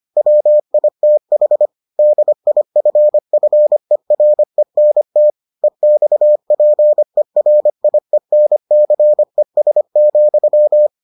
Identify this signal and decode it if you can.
Morse code, heard as a keyed tone, about 25 wpm, 600 Hz. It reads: WITH DIFFERENT EXPERIENCES,